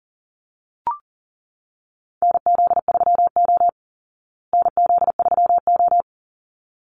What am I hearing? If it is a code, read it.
Morse code, heard as a keyed tone, about 40 wpm, 700 Hz. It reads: NZ3O NZ3O
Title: Call Signs Repeated 3x - 40wpm